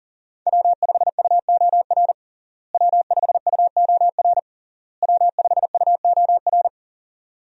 Morse code, heard as a keyed tone, about 40 wpm, 700 Hz. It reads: W5UOR W5UOR W5UOR
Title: USA Call Signs - 5 Characters - Repeated 3x - 40wpm